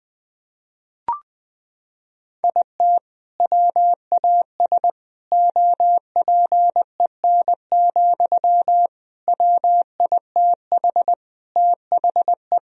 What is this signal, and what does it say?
Morse code, heard as a keyed tone, about 21 wpm, 700 Hz. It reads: IT WAS OPEN, WITH THE
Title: Sentences from Top 1000 Words - Review Part 4 - 20wpm